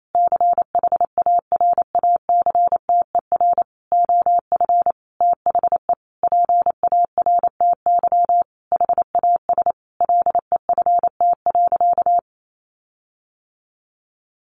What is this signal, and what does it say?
Morse code, heard as a keyed tone, about 28 wpm, 700 Hz. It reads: CHARACTER OF THE PARTY HAS LEFT.
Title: Sentences from Top 1500 Words 28wpm